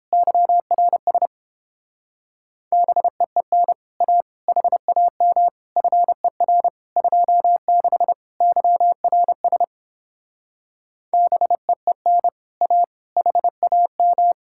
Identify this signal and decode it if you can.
Morse code, heard as a keyed tone, about 29 wpm, 700 Hz. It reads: YRS BEEN A HAM FER 26 YRS BEEN A HAM